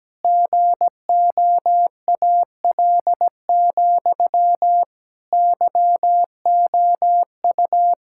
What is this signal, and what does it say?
Morse code, heard as a keyed tone, about 17 wpm, 700 Hz. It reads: GOAL, YOU
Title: Sentences from Top 1000 Words - Review Part 6 - 17wpm